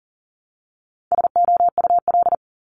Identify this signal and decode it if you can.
Morse code, heard as a keyed tone, about 41 wpm, 700 Hz. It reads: SOUL
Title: Multi-letter Phonemes 40wpm